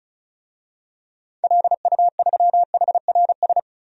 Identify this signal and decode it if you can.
Morse code, heard as a keyed tone, about 35 wpm, 700 Hz. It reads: LU3HRS
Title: DXCC-100 Call Signs - 6 Characters - 35wpm